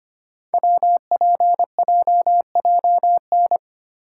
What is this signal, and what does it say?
Morse code, heard as a keyed tone, about 26 wpm, 700 Hz. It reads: WPJJN
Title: Code Groups - 5 Letters - 25wpm